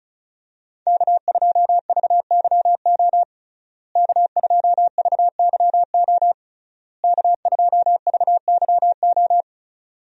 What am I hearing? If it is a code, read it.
Morse code, heard as a keyed tone, about 36 wpm, 700 Hz. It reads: K2VYO K2VYO K2VYO
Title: USA Call Signs - 5 Characters - Repeated 3x - 35wpm